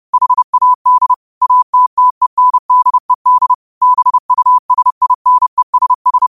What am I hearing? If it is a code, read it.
Morse code, heard as a keyed tone, about 30 wpm, 1000 Hz. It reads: HAD ATTENDED BUSINESS
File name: qotdfiles / 30wpm